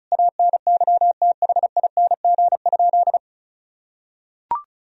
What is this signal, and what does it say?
Morse code, heard as a keyed tone, about 34 wpm, 700 Hz. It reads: ANYTHING?
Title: Sentences from Top 1000 Words - Review Part 4 - 35wpm